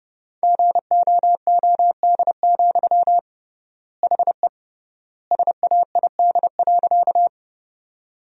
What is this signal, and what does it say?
Morse code, heard as a keyed tone, about 30 wpm, 700 Hz. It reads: GOOD, HE SAID.
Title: 3x Word-Spacing - Sentences from Top 200 Words - 30wpm